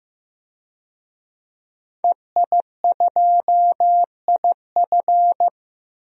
Morse code, heard as a keyed tone, about 15 wpm, 700 Hz. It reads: EI2IF
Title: DXCC-100 Call Signs - 5 Characters - 15wpm